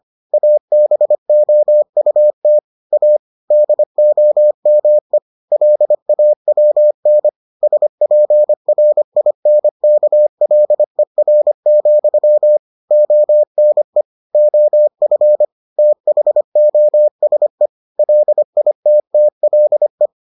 Morse code, heard as a keyed tone, about 25 wpm, 600 Hz. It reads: ABOUT A DOME LAWN SPRINKLER, ONE OF THOSE LITTLE